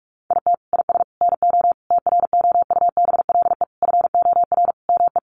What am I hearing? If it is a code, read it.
Morse code, heard as a keyed tone, about 44 wpm, 700 Hz. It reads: IT IS NO TROUBLE FOR ME
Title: Sentences from Top 1200 Words 45wpm